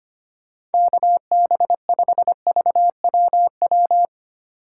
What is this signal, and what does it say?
Morse code, heard as a keyed tone, about 25 wpm, 700 Hz. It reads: KB5VWW
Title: USA Call Signs - 6 Characters - 3x Character Spacing - 25wpm